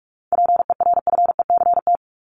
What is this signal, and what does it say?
Morse code, heard as a keyed tone, about 44 wpm, 700 Hz. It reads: PERFECT
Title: Top 751-1000 Words 45wpm